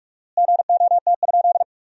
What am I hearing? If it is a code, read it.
Morse code, heard as a keyed tone, about 46 wpm, 700 Hz. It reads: GOT?
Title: 4x Word-Spacing - Sentences from Top 300 Words - 45wpm